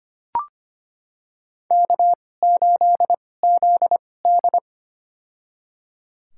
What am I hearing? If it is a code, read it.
Morse code, heard as a keyed tone, about 25 wpm, spaced as for 18 wpm, 700 Hz. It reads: K8ZD
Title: USA Call Signs - 4 Characters - 2x Character Spacing - 25wpm